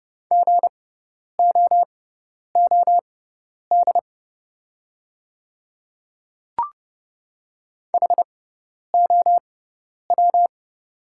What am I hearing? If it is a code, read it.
Morse code, heard as a keyed tone, about 30 wpm, spaced as for 10 wpm, 700 Hz. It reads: GOOD HOW
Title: 6x Character Spacing - Top 300 Words - 30wpm